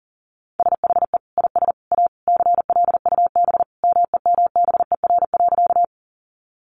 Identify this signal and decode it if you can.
Morse code, heard as a keyed tone, about 40 wpm, 700 Hz. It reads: SHE IS A CLUB MEMBER.
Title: Sentences from Top 1200 Words 40wpm